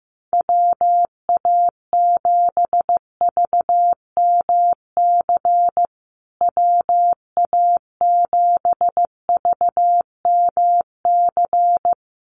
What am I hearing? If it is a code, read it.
Morse code, heard as a keyed tone, about 15 wpm, 700 Hz. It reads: WA7VMC WA7VMC
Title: USA Call Signs Repeated 3x 15wpm